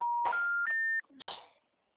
Special information tone.
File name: out- -2033- - - .39744